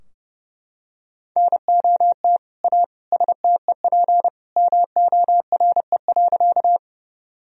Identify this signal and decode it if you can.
Morse code, heard as a keyed tone, about 30 wpm, 700 Hz. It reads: NOT A STEP MORE.
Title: Sentences from Top 1000 Words - Review Part 2 - 30wpm